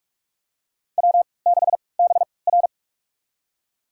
Morse code, heard as a keyed tone, about 48 wpm, spaced as for 26 wpm, 690 Hz. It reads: W6BR
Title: USA Call Signs - 4 Characters - 3x Character Spacing - 45wpm